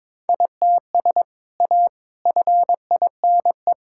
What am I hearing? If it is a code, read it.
Morse code, heard as a keyed tone, about 21 wpm, 700 Hz. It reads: ITS A FINE
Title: Sentences from Top 1000 Words - Review Part 6 - 22wpm